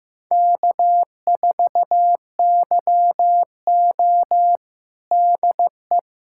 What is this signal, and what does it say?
Morse code, heard as a keyed tone, about 16 wpm, 700 Hz. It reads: K4YO DE
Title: USA Callsign DE Callsign - 4 Characters - 15wpm